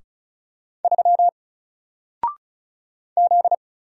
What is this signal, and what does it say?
Morse code, heard as a keyed tone, about 36 wpm, 700 Hz. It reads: 3 Z
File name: Single-Letter-Number-35wpm